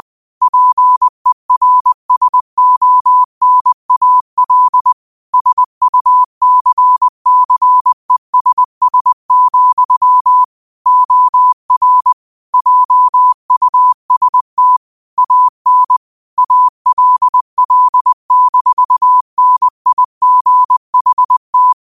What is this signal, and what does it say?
Morse code, heard as a keyed tone, about 20 wpm, 1000 Hz. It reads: PERSONAL SUCCESS, OR JUST AN ALL-NIGHT
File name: qotdfiles / 20wpm